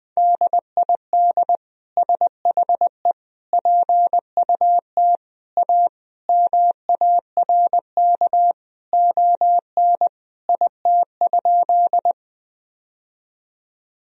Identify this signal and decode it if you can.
Morse code, heard as a keyed tone, about 20 wpm, 700 Hz. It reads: DID SHE PUT A MARK ON IT?
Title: Sentences from Top 1000 Words - Review Part 1 - 20wpm